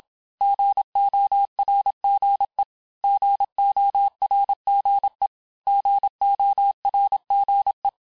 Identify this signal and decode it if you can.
Morse code, heard as a keyed tone, about 27 wpm, 800 Hz. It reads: GORGE GORGE GORGE
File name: cours-du-mercredi / audios / cours-869-texte-LES-MOTS-vitesse-25-MPM